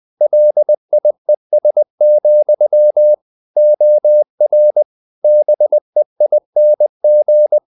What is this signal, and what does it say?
Morse code, heard as a keyed tone, about 20 wpm, 600 Hz. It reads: LIES, OR BEING
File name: CW / Practice / If20